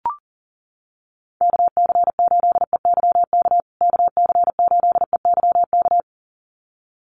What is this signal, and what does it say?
Morse code, heard as a keyed tone, about 40 wpm, 700 Hz. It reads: KC8EYK KC8EYK
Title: USA Call Signs Repeated 2x 40wpm